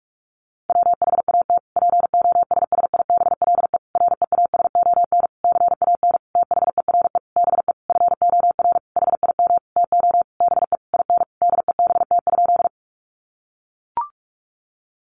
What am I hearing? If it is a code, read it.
Morse code, heard as a keyed tone, about 44 wpm, 700 Hz. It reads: WHAT POSSIBLE REASON CAN THERE BE FOR HIM TO BE IN DEBT?
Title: Sentences from Top 1300 Words 45wpm